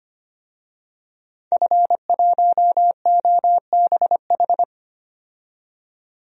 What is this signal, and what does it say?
Morse code, heard as a keyed tone, about 25 wpm, 700 Hz. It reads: F1OBH
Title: DXCC-100 Call Signs - 5 Characters - 25wpm